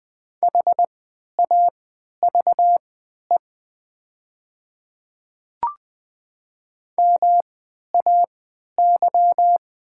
Morse code, heard as a keyed tone, about 20 wpm, spaced as for 11 wpm, 700 Hz. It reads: HAVE MAY
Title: 3x Character Spacing - Top 200 Words - 20wpm